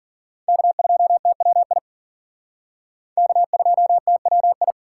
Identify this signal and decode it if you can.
Morse code, heard as a keyed tone, about 51 wpm, 700 Hz. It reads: K2TWI K2TWI